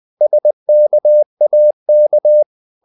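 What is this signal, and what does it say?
Morse code, heard as a keyed tone, about 20 wpm, 600 Hz. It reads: SKAK
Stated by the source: Morse code practice of short Danish words at 20 wpm, from CW med Gnister.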